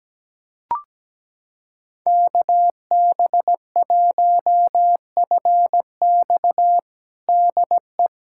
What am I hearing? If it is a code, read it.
Morse code, heard as a keyed tone, about 17 wpm, 700 Hz. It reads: KB1FX DE
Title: USA Callsign DE Callsign - 5 Characters - 17wpm